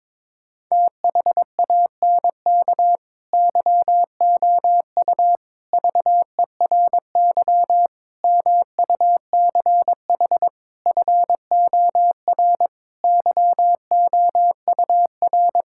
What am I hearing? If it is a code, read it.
Morse code, heard as a keyed tone, about 22 wpm, 700 Hz. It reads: THANK YOU VERY MUCH FOR YOUR
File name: practice / 22wpm / Sentences with 4-Word Phrases - 22wpm